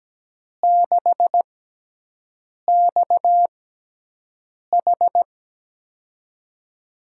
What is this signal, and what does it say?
Morse code, heard as a keyed tone, about 17 wpm, spaced as for 6 wpm, 700 Hz. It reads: 6XH
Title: USA Call Signs - 5 Characters - 6x Character Spacing - 17wpm